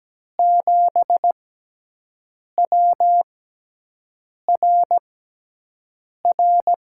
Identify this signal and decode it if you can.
Morse code, heard as a keyed tone, about 17 wpm, spaced as for 6 wpm, 700 Hz. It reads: 7WRR
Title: USA Call Signs - 5 Characters - 6x Character Spacing - 17wpm